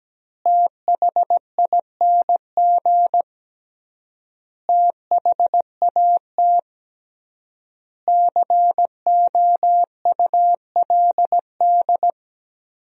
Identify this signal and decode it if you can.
Morse code, heard as a keyed tone, about 17 wpm, 700 Hz. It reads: THING THAT COULD
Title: 3x Word-Spacing - Sentences from Top 400 Words - 17wpm